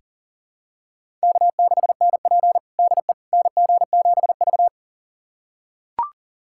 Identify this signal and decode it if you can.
Morse code, heard as a keyed tone, about 37 wpm, 700 Hz. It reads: K6NP DE NG7V
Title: USA Callsign DE Callsign - 4 Characters - 40wpm